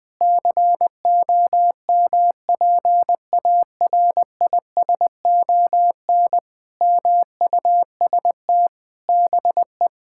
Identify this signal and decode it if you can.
Morse code, heard as a keyed tone, about 20 wpm, 700 Hz. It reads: COMPARISON MUST BE
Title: Sentences from Top 1600 Words 20wpm